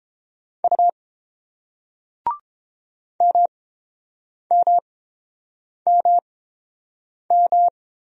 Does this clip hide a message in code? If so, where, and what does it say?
Yes: Morse code from 0.6 to 7.7 s, heard as a keyed tone, about 30 wpm, spaced as for 7 wpm, 700 Hz. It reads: U MMMM